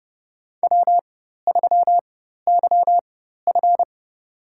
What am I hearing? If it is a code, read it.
Morse code, heard as a keyed tone, about 31 wpm, spaced as for 14 wpm, 700 Hz. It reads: W3YF
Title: USA Call Signs - 4x Character Spacing - 30wpm